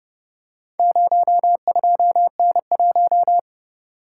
Morse code, heard as a keyed tone, about 29 wpm, 700 Hz. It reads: 02N1